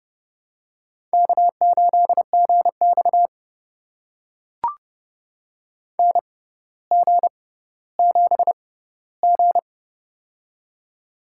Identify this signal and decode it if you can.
Morse code, heard as a keyed tone, about 31 wpm, spaced as for 16 wpm, 700 Hz. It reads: K8GX NG7G
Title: USA Call Signs - 4 Characters - 6x Character Spacing - 30wpm